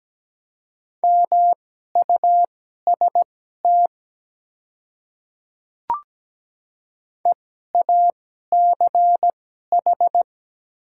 Morse code, heard as a keyed tone, about 17 wpm, spaced as for 12 wpm, 700 Hz. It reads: MUST EACH
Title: 2x Character Spacing - Top 200 Words - 17wpm